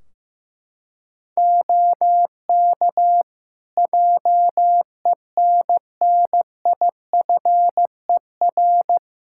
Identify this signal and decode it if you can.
Morse code, heard as a keyed tone, about 15 wpm, 700 Hz. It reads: OK JENNIFER
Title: Practice QSO - Exchange 3 - More Info - 15wpm